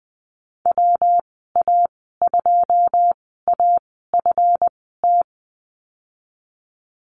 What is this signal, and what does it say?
Morse code, heard as a keyed tone, about 20 wpm, spaced as for 14 wpm, 700 Hz. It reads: WA2AFT
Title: 147 Lesson 24g - Letter y - Call Signs 20wpm